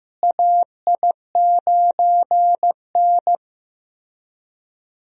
Morse code, heard as a keyed tone, about 15 wpm, 700 Hz. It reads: AI9N
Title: USA Call Signs - 4 Characters - Repeated 3x - 15wpm